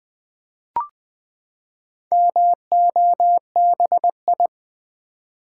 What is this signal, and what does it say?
Morse code, heard as a keyed tone, about 21 wpm, 700 Hz. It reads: MOBI
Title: Build A Word - 4 Letters At A Time - 8-Letter Words - 20wpm